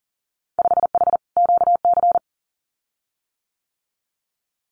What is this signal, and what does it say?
Morse code, heard as a keyed tone, about 40 wpm, 700 Hz. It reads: 5H QC
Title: Winter Field Day Exchanges Encore 40wpm